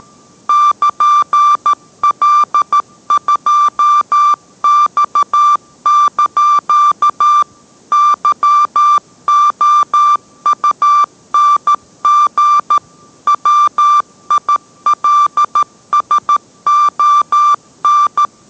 Morse code, heard as a keyed tone, about 13 wpm, 1180 Hz. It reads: <KN>L2X) YOUNG WILSON